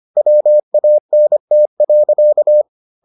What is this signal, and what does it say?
Morse code, heard as a keyed tone, about 25 wpm, 600 Hz. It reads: WANT.